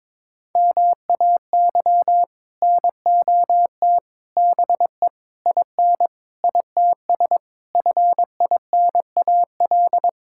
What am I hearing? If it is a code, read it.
Morse code, heard as a keyed tone, about 22 wpm, 700 Hz. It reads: MAY NOT BE IN ITS FINAL
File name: practice / 22wpm / Sets of 7 Words - 22wpm